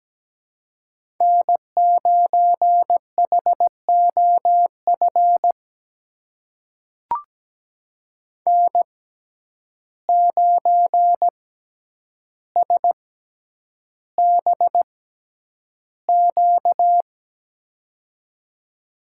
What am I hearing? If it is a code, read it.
Morse code, heard as a keyed tone, about 17 wpm, spaced as for 9 wpm, 700 Hz. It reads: N9HOF N9SBQ
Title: USA Call Signs - 5 Characters - 6x Character Spacing - 17wpm